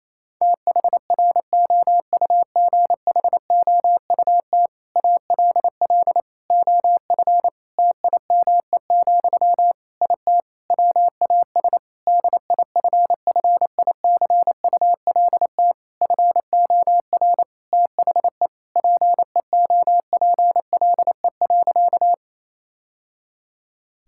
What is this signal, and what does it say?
Morse code, heard as a keyed tone, about 27 wpm, 700 Hz. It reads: THROUGHOUT ALL OF TIME, IT WAS DIFFICULT FOR THE PEOPLE.
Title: Sentences from Top 1000 Words - Review Part 5 - 28wpm